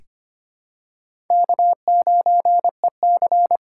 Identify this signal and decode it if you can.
Morse code, heard as a keyed tone, about 25 wpm, 700 Hz. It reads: K9EC
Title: USA Call Signs - 4 Characters - 3x Character Spacing - 25wpm